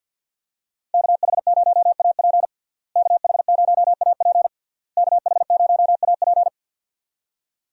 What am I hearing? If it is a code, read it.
Morse code, heard as a keyed tone, about 51 wpm, 700 Hz. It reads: KH0AP KH0AP KH0AP
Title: USA Call Signs - 5 Characters - Repeated 3x - 50wpm